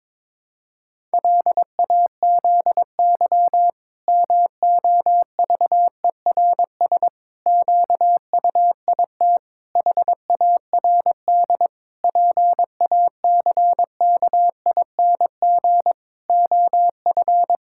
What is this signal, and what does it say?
Morse code, heard as a keyed tone, about 22 wpm, 700 Hz. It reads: LAZY MOVERS QUIT HARD PACKING OF
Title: Pangram Challenge v2 - 22wpm